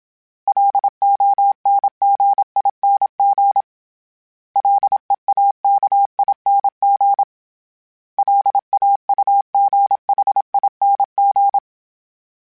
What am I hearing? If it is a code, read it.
Morse code, heard as a keyed tone, about 27 wpm, 800 Hz. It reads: LONGING LEAKING LAUGHING
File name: Special INGs-mp3s / Special INGs-mp3s / Sngl Ing @25